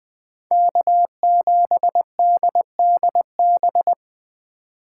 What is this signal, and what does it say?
Morse code, heard as a keyed tone, about 20 wpm, 700 Hz. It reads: K7DDB
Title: Speed Racing - USA 1x3 Call Signs 17wpm